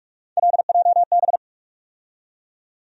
Morse code, heard as a keyed tone, about 44 wpm, 700 Hz. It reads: LJB
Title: USA Call Sign Postfixes - 3 Letters 45wpm